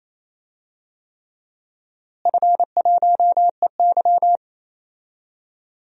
Morse code, heard as a keyed tone, about 28 wpm, 700 Hz. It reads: F1EY
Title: DXCC-100 Call Signs - 4 Characters - Repeated 3x - 28wpm